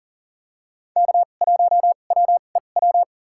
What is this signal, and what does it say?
Morse code, heard as a keyed tone, about 40 wpm, spaced as for 29 wpm, 700 Hz. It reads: K1WEW
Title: USA Call Signs - 5 Characters - 2x Character Spacing - 40wpm